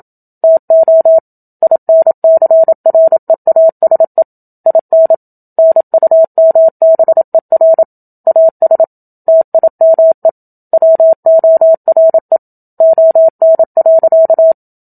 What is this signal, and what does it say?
Morse code, heard as a keyed tone, about 27 wpm, 650 Hz. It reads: TO INCREASE IN NUMBER AS TIME WORE ON.
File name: Book Two / 01-Chapter 1 - 27 WPM